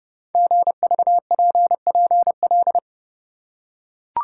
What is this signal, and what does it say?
Morse code, heard as a keyed tone, about 30 wpm, 700 Hz. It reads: GVPPL
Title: Code Groups - 5 Letters - 30wpm